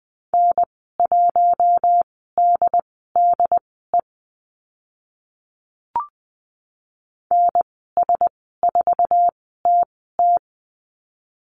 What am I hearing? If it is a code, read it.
Morse code, heard as a keyed tone, about 20 wpm, spaced as for 14 wpm, 700 Hz. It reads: N1DDE NS4TT
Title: 069 Lesson 12g - Letter d - Call Signs 20wpm